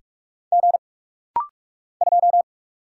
Morse code, heard as a keyed tone, about 45 wpm, 700 Hz. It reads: G 2
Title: Single Letter-Number Alt - Rapid-Fire 45wpm